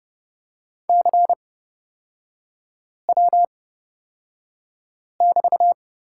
Morse code, heard as a keyed tone, about 31 wpm, spaced as for 5 wpm, 690 Hz. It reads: CW<BT>
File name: Lesson 38e - Letter q - Three Characters 30wpm